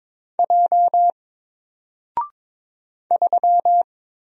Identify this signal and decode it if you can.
Morse code, heard as a keyed tone, about 22 wpm, 700 Hz. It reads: J 3
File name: Single-Letter-Number-22wpm